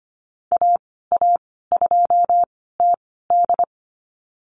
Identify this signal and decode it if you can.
Morse code, heard as a keyed tone, about 25 wpm, spaced as for 16 wpm, 700 Hz. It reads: AA2TD
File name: Lesson 15g - Number 5 - Call Signs 25wpm